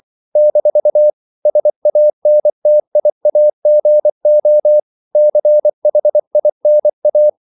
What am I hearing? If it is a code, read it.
Morse code, heard as a keyed tone, about 24 wpm, 600 Hz. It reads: - SANTIAGO CHINA
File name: CW / Practice / NationalCapitals24